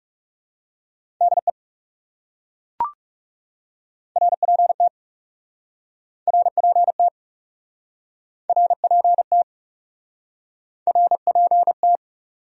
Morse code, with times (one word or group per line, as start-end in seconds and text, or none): 1.2-1.5 DE
4.2-4.9 RPT
6.3-7.1 RPT
8.5-9.4 RPT
10.9-12.0 RPT